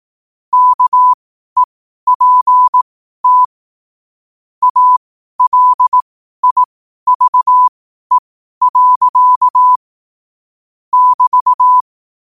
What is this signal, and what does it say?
Morse code, heard as a keyed tone, about 18 wpm, spaced as for 12 wpm, 1000 Hz. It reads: KEPT ALIVE. <BT>
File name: qotdfiles / 12wpm